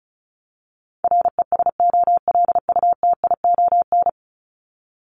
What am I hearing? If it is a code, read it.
Morse code, heard as a keyed tone, about 34 wpm, 700 Hz. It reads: RESOLUTION
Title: Top 1501-1750 Words 35wpm